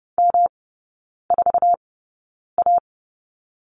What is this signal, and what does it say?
Morse code, heard as a keyed tone, about 30 wpm, spaced as for 9 wpm, 700 Hz. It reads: M4A
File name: Lesson 18e - Letter m - Three Characters 30wpm